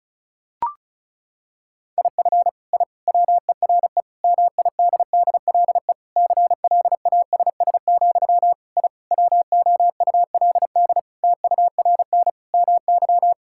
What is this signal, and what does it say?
Morse code, heard as a keyed tone, about 36 wpm, 700 Hz. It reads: IF I WERE MIDDLE CLASS, I WOULD TURN MY
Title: Sentences from Top 1000 Words - Review Part 6 - 35wpm